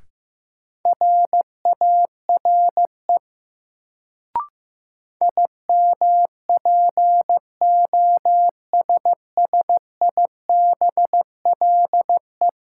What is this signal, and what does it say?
Morse code, heard as a keyed tone, about 15 wpm, 700 Hz. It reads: RARE IMPOSSIBLE
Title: Sentences from Top 1100 Words - New Word Drill - 15wpm